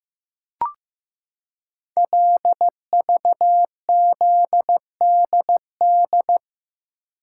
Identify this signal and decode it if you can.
Morse code, heard as a keyed tone, about 15 wpm, 700 Hz. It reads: LVZDD
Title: Code Groups - 5 Letters - 15wpm